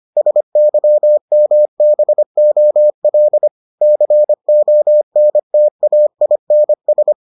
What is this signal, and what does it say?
Morse code, heard as a keyed tone, about 25 wpm, 600 Hz. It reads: SYMBOL CONTAINS